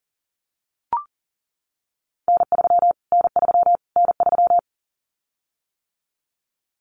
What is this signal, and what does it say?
Morse code, heard as a keyed tone, about 40 wpm, 700 Hz. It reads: N3 N3 N3
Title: Callsign Prefixes for Top 25 DXCC - Repeated 3x - 40wpm